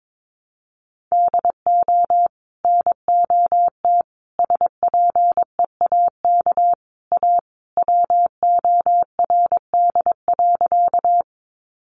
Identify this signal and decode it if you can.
Morse code, heard as a keyed tone, about 22 wpm, 700 Hz. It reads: DO NOT SPEAK A WORD.
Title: Sentences from Top 1100 Words 22wpm